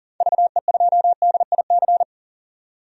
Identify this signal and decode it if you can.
Morse code, heard as a keyed tone, about 37 wpm, 700 Hz. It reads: VE2DIC
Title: DXCC-100 Call Signs - 6 Characters - 40wpm